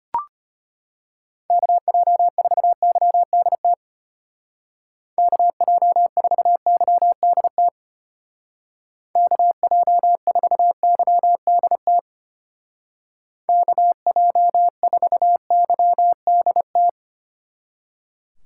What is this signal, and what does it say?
Morse code, heard as a keyed tone, about 30 wpm, 700 Hz. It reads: KJ4YDT KJ4YDT KJ4YDT KJ4YDT
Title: Speed Racing - USA 2x3 Call Signs 25wpm